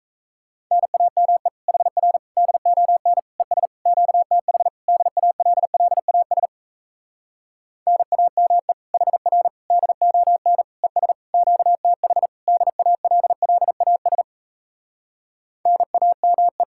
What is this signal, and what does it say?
Morse code, heard as a keyed tone, about 38 wpm, 700 Hz. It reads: NAME HR DON ES QTH DALLAS NAME HR DON ES QTH DALLAS NAME